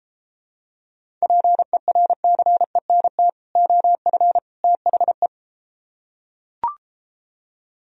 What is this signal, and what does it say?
Morse code, heard as a keyed tone, about 34 wpm, 700 Hz. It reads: PERCENT OF THE